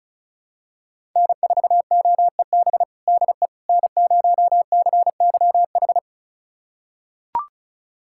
Morse code, heard as a keyed tone, about 36 wpm, 700 Hz. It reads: N4OEB DE N0CYH
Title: USA Callsign DE Callsign 35wpm